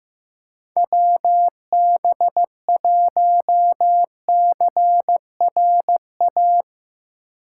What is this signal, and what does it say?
Morse code, heard as a keyed tone, about 15 wpm, 700 Hz. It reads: WB1CRA
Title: USA Call Signs - 4x Character Spacing - 15wpm